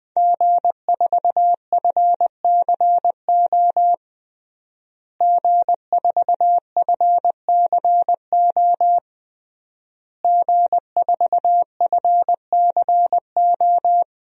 Morse code, heard as a keyed tone, about 20 wpm, 700 Hz. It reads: G4FCO G4FCO G4FCO
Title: DXCC-100 Call Signs - 5 Characters - Repeated 3x - 20wpm